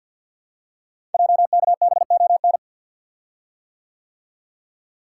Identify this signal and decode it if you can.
Morse code, heard as a keyed tone, about 50 wpm, 700 Hz. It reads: JKBON
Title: Code Groups - 5 Letter:Numbers - 50wpm